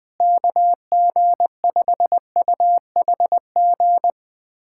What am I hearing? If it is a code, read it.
Morse code, heard as a keyed tone, about 21 wpm, 700 Hz. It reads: KG5UHG
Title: USA Callsign DE Callsign 20wpm